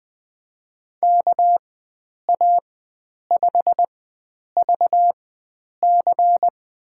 Morse code, heard as a keyed tone, about 21 wpm, spaced as for 9 wpm, 700 Hz. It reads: KA5VC
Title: USA Call Signs - 4x Character Spacing - 20wpm